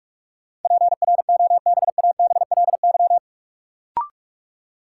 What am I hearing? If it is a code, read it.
Morse code, heard as a keyed tone, about 46 wpm, 700 Hz. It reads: PROBABLY
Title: Sentences from Top 700 Words - New Word Drill - 45wpm